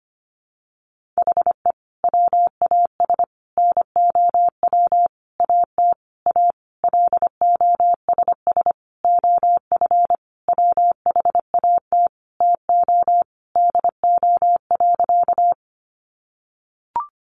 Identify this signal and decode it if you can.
Morse code, heard as a keyed tone, about 25 wpm, 700 Hz. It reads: HE WAS NOW AT A LOSS OF WHAT TO DO.
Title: Sentences from Top 1400 Words 25wpm